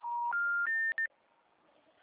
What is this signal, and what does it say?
Special information tone.